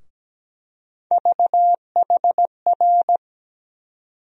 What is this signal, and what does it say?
Morse code, heard as a keyed tone, about 17 wpm, 700 Hz. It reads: VHR
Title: USA Call Sign Postfixes - 3 Letters 17wpm